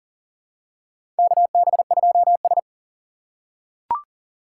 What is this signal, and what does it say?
Morse code, heard as a keyed tone, about 37 wpm, 700 Hz. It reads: KB2S
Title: K1USN SST QSO Practice - 40wpm